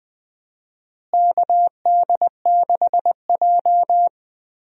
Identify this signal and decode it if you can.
Morse code, heard as a keyed tone, about 21 wpm, 700 Hz. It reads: KD6J
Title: USA Call Signs - 4 Characters - Repeated 3x - 20wpm